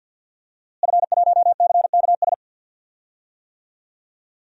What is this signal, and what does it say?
Morse code, heard as a keyed tone, about 46 wpm, 700 Hz. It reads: F1CKS
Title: DXCC-100 Call Signs - 5 Characters - Repeated 3x - 50wpm